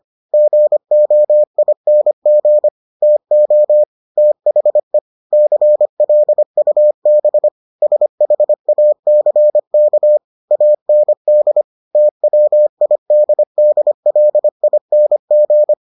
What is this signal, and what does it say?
Morse code, heard as a keyed tone, about 25 wpm, 600 Hz. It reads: GOING TO THE CLUB SHACK AND TWIDDLING